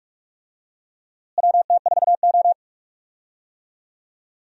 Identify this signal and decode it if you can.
Morse code, heard as a keyed tone, about 46 wpm, 700 Hz. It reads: WT4O
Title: USA Call Signs - 4 Characters - Repeated 2x - 45wpm